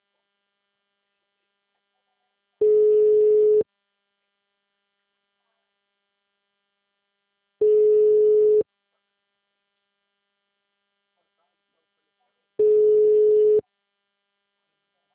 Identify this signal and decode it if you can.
Ringback tone (Europe, 425 Hz), with speech.
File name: out- -2017- - - .84170